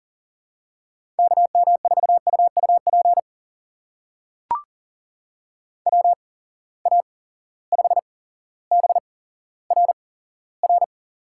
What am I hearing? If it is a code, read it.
Morse code, heard as a keyed tone, about 39 wpm, spaced as for 17 wpm, 700 Hz. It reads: KM4UUP WA5BRR
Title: USA Call Signs - 6 Characters - 8x Character Spacing - 40wpm